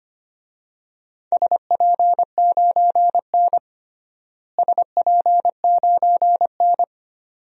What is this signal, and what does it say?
Morse code, heard as a keyed tone, about 25 wpm, 700 Hz. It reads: SP9N SP9N
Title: DXCC-100 Call Signs - 4 Characters - Repeated 2x - 25wpm